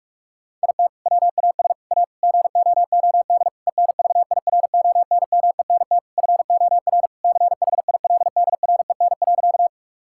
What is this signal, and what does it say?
Morse code, heard as a keyed tone, about 46 wpm, 700 Hz. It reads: IT WAS A GOOD ENVIRONMENT FOR CHILDREN.
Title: Sentences from Top 1000 Words - Review Part 5 - 45wpm